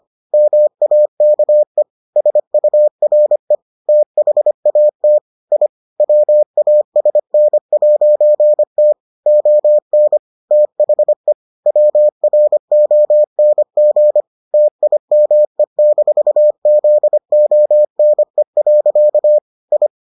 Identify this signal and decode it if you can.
Morse code, heard as a keyed tone, about 25 wpm, 600 Hz. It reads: MAKE SURE THAT I WASN'T ON THE WRONG TIME-ZONE. I